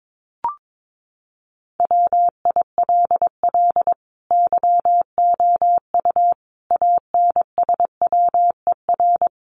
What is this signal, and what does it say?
Morse code, heard as a keyed tone, about 22 wpm, 700 Hz. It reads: WILL YOU ANSWER
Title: Sentences from Top 400 Words 22wpm